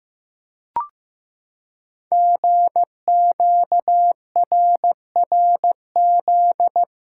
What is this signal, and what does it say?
Morse code, heard as a keyed tone, about 15 wpm, 700 Hz. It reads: GQRRZ
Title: Code Groups - 5 Letters - 15wpm